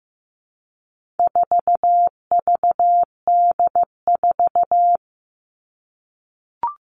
Morse code, heard as a keyed tone, about 15 wpm, 700 Hz. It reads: 4VD4
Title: All Cut Numbers - 4-Digit 15wpm